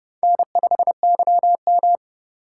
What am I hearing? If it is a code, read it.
Morse code, heard as a keyed tone, about 36 wpm, 700 Hz. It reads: N5YM
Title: Speed Racing - USA 1x2 Call Signs 30wpm